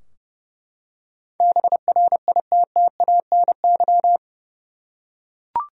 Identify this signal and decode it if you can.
Morse code, heard as a keyed tone, about 30 wpm, 700 Hz. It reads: BRITTANY
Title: USA Common Names - 30wpm